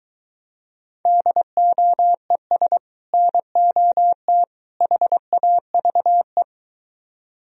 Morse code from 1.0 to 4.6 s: DOES NOT

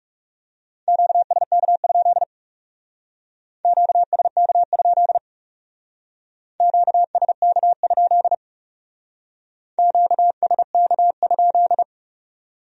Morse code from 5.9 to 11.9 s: QSK? QSK?

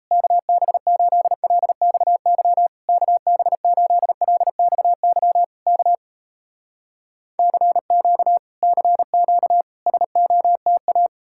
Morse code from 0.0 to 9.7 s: KB8LXY KB8LXY K CQ CQ

